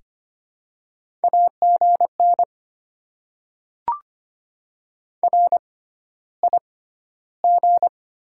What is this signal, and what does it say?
Morse code, heard as a keyed tone, about 25 wpm, spaced as for 13 wpm, 700 Hz. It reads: AGN RIG
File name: Top 20 QSO Elements - 6x Character Spacing - 25wpm